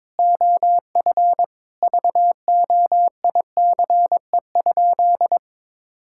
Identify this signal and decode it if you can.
Morse code, heard as a keyed tone, about 22 wpm, 700 Hz. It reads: OF VOICE?
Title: Sentences from Top 1600 Words 22wpm